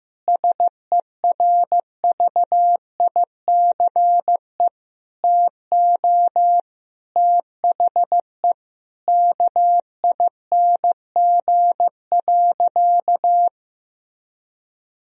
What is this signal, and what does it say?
Morse code, heard as a keyed tone, about 15 wpm, 700 Hz. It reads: SERVICE TO THE KING.
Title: Sentences from Top 1000 Words - Review Part 4 - 15wpm